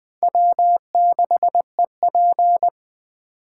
Morse code, heard as a keyed tone, about 19 wpm, 700 Hz. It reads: W6EP